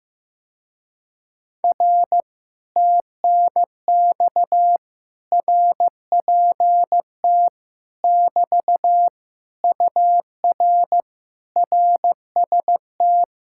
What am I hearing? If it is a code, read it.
Morse code, heard as a keyed tone, about 15 wpm, 700 Hz. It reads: R TNX RPT <BT> UR RST
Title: Speed-Racing Common QSO Phrases Encore - 15wpm